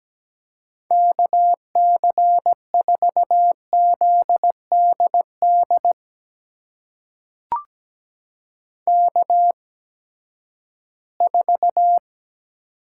Morse code, heard as a keyed tone, about 17 wpm, spaced as for 12 wpm, 700 Hz. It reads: KC4ZDD K4
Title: USA Call Signs - 8x Character Spacing - 17wpm